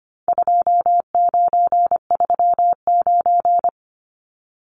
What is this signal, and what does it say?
Morse code, heard as a keyed tone, about 25 wpm, 700 Hz. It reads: 2939
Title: Numbers - 4-Digit 25wpm